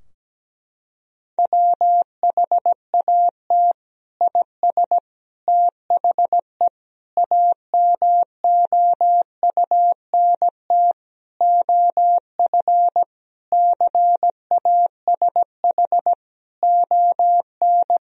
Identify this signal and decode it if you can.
Morse code, heard as a keyed tone, about 17 wpm, 700 Hz. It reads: WHAT IS THE AMOUNT OF CASH ON
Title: Sentences from Top 1000 Words - Review Part 4 - 17wpm